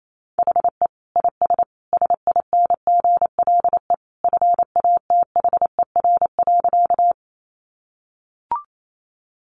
Morse code, heard as a keyed tone, about 28 wpm, 700 Hz. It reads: HE IS SINGLE FATHER.
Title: Sentences from Top 700 Words 28wpm